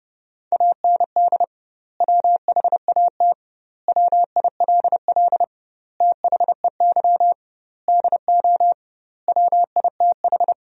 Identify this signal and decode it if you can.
Morse code, heard as a keyed tone, about 30 wpm, 700 Hz. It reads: AND WHAT WILL THEY DO WITH
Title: 2x Word-Spacing - Sentences from Top 200 Words - 30wpm